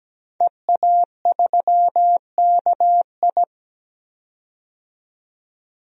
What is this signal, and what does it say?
Morse code, heard as a keyed tone, about 17 wpm, 700 Hz. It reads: EA3KI
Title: DXCC-100 Call Signs - 5 Characters - 17wpm